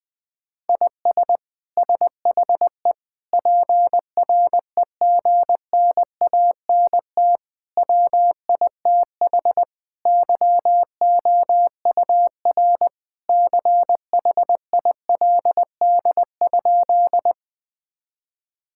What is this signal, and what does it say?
Morse code, heard as a keyed tone, about 20 wpm, 700 Hz. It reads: IS SHE PREGNANT WITH YOUR CHILD?
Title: Sentences from Top 1000 Words - Review Part 6 - 20wpm